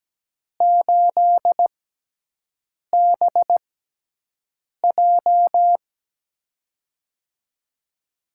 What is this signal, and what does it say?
Morse code, heard as a keyed tone, about 17 wpm, spaced as for 6 wpm, 700 Hz. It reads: 8BJ
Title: USA Call Signs - 6x Character Spacing - 17wpm